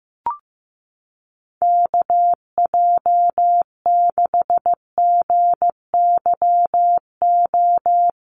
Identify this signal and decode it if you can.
Morse code, heard as a keyed tone, about 15 wpm, 700 Hz. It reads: KJ6GYO
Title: USA Call Signs Repeated 2x 15wpm